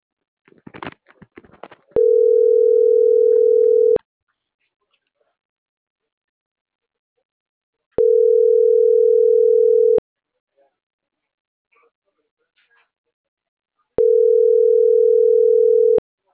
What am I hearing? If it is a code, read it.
Ringback tone (North America), 440 + 480 Hz.